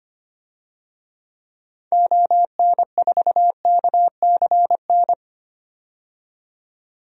Morse code, heard as a keyed tone, about 25 wpm, 700 Hz. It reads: ON4KCN
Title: DXCC-100 Call Signs - 6 Characters - 25wpm